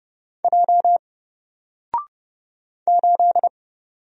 Morse code, heard as a keyed tone, about 31 wpm, 700 Hz. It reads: J 8
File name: Single-Letter-Number-30wpm